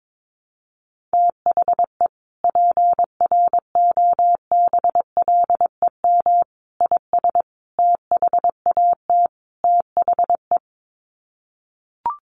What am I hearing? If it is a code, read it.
Morse code, heard as a keyed tone, about 22 wpm, 700 Hz. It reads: THE PROBLEM IS THAT THE
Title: Sets of 5 Words Encore2 22wpm